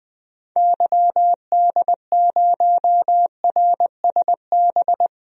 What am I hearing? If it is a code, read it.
Morse code, heard as a keyed tone, about 20 wpm, 700 Hz. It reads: YD0RSB
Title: DXCC-100 Call Signs - 6 Characters - Repeated 3x - 20wpm